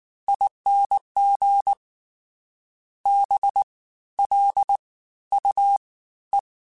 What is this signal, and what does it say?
Morse code, heard as a keyed tone, about 19 wpm, 800 Hz. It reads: ING B L U E
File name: Special INGs-mp3s / Special INGs-mp3s / Fast ing @18